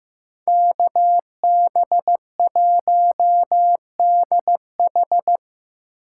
Morse code, heard as a keyed tone, about 18 wpm, 700 Hz. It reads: KB1DH
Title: Speed Racing - USA 2x2 Call Signs 15wpm